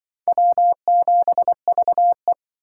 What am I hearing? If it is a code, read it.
Morse code, heard as a keyed tone, about 25 wpm, 700 Hz. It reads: W7VE
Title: Speed Racing - USA 1x2 Call Signs 20wpm